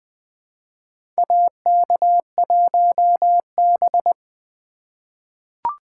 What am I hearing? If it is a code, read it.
Morse code, heard as a keyed tone, about 20 wpm, 700 Hz. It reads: AK1B
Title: USA Call Signs - 4 Characters - 3x Character Spacing - 20wpm